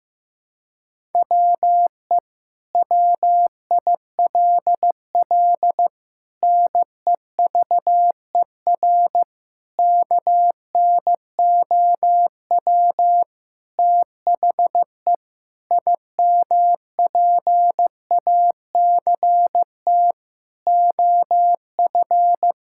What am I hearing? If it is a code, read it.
Morse code, heard as a keyed tone, about 15 wpm, 700 Hz. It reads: WE WILL NEVER KNOW THE IMPACT OF
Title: Sentences from Top 1000 Words - Review Part 6 - 15wpm